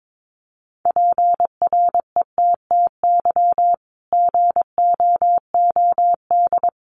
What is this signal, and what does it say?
Morse code, heard as a keyed tone, about 22 wpm, 700 Hz. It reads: PRETTY GOOD
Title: Sentences from Top 1400 Words 22wpm